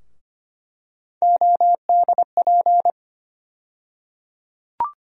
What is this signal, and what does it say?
Morse code, heard as a keyed tone, about 25 wpm, 700 Hz. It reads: ODP
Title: USA Call Sign Postfixes - 3 Letters 25wpm